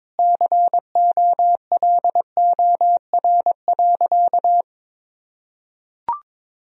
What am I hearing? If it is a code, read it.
Morse code, heard as a keyed tone, about 22 wpm, 700 Hz. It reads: COLOR.
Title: Sentences from Top 1000 Words - Review Part 2 - 22wpm